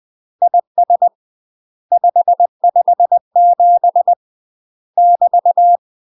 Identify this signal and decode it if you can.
Morse code, heard as a keyed tone, about 20 wpm, 700 Hz. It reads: IS 557 <BT>
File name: UK MASTER QSOs TRAINING / 20WPM / UK MASTER QSOs 0069